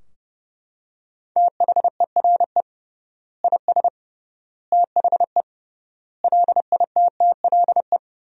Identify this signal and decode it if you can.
Morse code, heard as a keyed tone, about 32 wpm, 700 Hz. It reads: THERE IS THE LITTLE
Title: 3x Word-Spacing - Sentences from Top 200 Words - 30wpm